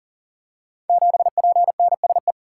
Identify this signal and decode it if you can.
Morse code, heard as a keyed tone, about 42 wpm, 690 Hz. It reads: 7PNSE
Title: Code Groups - 5 Letter:Numbers - 40wpm